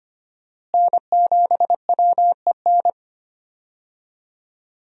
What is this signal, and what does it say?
Morse code, heard as a keyed tone, about 25 wpm, 700 Hz. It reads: N7WEN
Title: USA Call Signs - 5 Characters - Repeated 3x - 25wpm